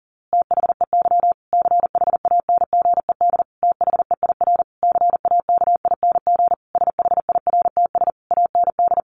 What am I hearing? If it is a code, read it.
Morse code, heard as a keyed tone, about 40 wpm, 700 Hz. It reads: THEY CHANGED THEIR CAKING SHIRTS AND
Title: A Sound Of Thunder 40wpm